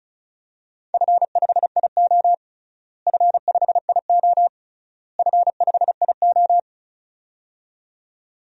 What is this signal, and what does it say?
Morse code, heard as a keyed tone, about 36 wpm, 700 Hz. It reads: F5IO F5IO F5IO
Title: DXCC-100 Call Signs - 4 Characters - Repeated 3x - 35wpm